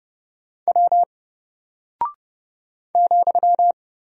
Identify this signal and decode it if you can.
Morse code, heard as a keyed tone, about 30 wpm, 700 Hz. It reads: W ,
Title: Single Character (Letter:Number:Punctuation:Prosign) Alt 30wpm